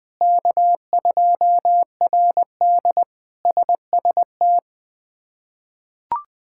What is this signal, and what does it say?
Morse code, heard as a keyed tone, about 21 wpm, 700 Hz. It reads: K2RD SST
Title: K1USN SST QSO Practice - 20wpm